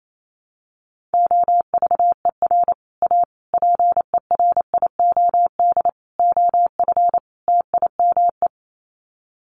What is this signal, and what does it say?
Morse code, heard as a keyed tone, about 28 wpm, 700 Hz. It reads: OVER A PERIOD OF TIME
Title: Sets of 5 Words Encore2 28wpm